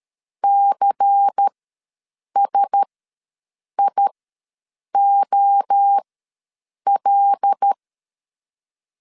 Morse code, heard as a keyed tone, about 13 wpm, spaced as for 7 wpm, 800 Hz. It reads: CSIOL
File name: LEZIONI / 5 LBVC / 5 LBVC ws 12wpm 3159